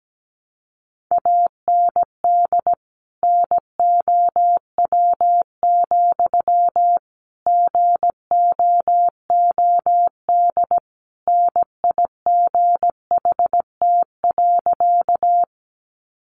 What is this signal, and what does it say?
Morse code, heard as a keyed tone, about 17 wpm, 700 Hz. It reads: AND NOW, GOOD NIGHT.
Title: Sentences from Top 300 Words 17wpm